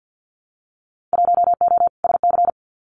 Morse code, heard as a keyed tone, about 50 wpm, 700 Hz. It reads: 1O SC
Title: Winter Field Day Exchanges 50wpm